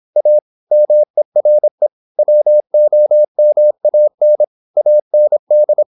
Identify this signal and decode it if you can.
Morse code, heard as a keyed tone, about 26 wpm, 600 Hz. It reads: A MERE WOMAN AND
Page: Never 26wpm | CW med Gnister